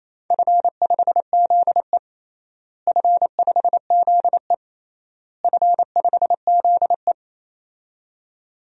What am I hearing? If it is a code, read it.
Morse code, heard as a keyed tone, about 28 wpm, 700 Hz. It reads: F5ZE F5ZE F5ZE
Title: DXCC-100 Call Signs - 4 Characters - Repeated 3x - 28wpm